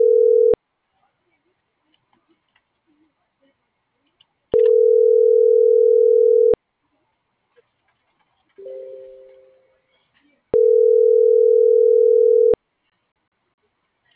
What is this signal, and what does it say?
Ringback tone (North America), 440 + 480 Hz.